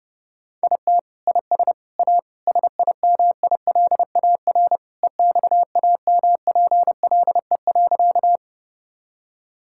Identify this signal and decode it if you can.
Morse code, heard as a keyed tone, about 32 wpm, 700 Hz. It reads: IT IS A SIMILAR EXAMPLE.
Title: Sentences from Top 1000 Words - Review Part 5 - 30wpm